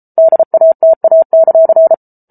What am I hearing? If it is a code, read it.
Morse code, heard as a keyed tone, about 34 wpm, 650 Hz. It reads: DATA;